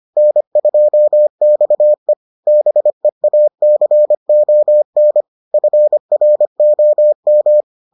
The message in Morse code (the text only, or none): N2XE BEACON FROM